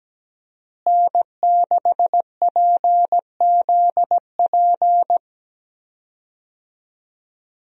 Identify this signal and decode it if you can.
Morse code, heard as a keyed tone, about 17 wpm, 700 Hz. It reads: N6PZP
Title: USA Call Signs - 5 Characters - Repeated 2x - 17wpm